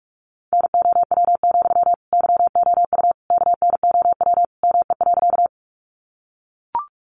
Morse code, heard as a keyed tone, about 44 wpm, 700 Hz. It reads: NOW, YOU KNOW ME.
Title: Sentences from Top 200 Words 45wpm